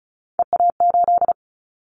Morse code, heard as a keyed tone, about 34 wpm, 700 Hz. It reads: EA8
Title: Callsign Prefixes for Top 100 DXCC - 35wpm